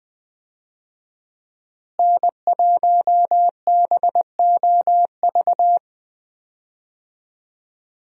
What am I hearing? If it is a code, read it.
Morse code, heard as a keyed tone, about 20 wpm, 700 Hz. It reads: N1BOV
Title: USA Call Signs - 5 Characters - Repeated 2x - 20wpm